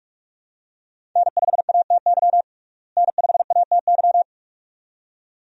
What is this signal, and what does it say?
Morse code, heard as a keyed tone, about 41 wpm, 700 Hz. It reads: N5ATY N5ATY
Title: USA Call Signs - 5 Characters - Repeated 2x - 45wpm